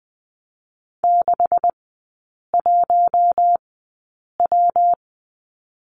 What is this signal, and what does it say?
Morse code, heard as a keyed tone, about 20 wpm, 700 Hz. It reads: 6 1 W
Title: 127 Lesson 21e - Number 6 - Three Characters 20wpm